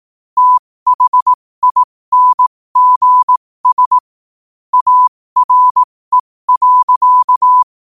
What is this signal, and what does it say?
Morse code, heard as a keyed tone, about 18 wpm, spaced as for 15 wpm, 1000 Hz. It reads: THINGS ARE.
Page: 15 WPM morse code quotes